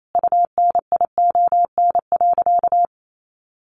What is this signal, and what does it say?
Morse code, heard as a keyed tone, about 28 wpm, 700 Hz. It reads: UNION.
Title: Sentences from Top 1400 Words 28wpm